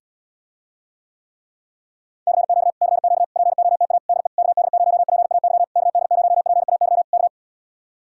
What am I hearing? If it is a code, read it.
Morse code, heard as a keyed tone, about 80 wpm, 700 Hz. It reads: CQ CQ POTA DE KD0PAY KD0PAY K